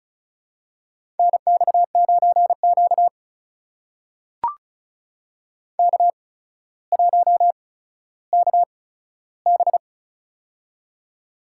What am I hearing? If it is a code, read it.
Morse code, heard as a keyed tone, about 36 wpm, spaced as for 15 wpm, 700 Hz. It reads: NX9Q K1KB
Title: USA Call Signs - 4 Characters - 8x Character Spacing - 35wpm